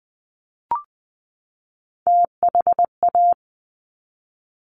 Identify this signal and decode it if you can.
Morse code, heard as a keyed tone, about 20 wpm, 700 Hz. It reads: THA
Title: Top Three Letter Combinations in Words 20wpm